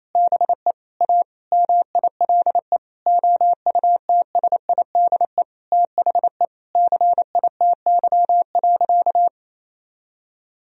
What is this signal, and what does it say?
Morse code, heard as a keyed tone, about 28 wpm, 700 Hz. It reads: BE A MILE OUTSIDE THE CITY.
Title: Sentences from Top 1000 Words - Review Part 4 - 28wpm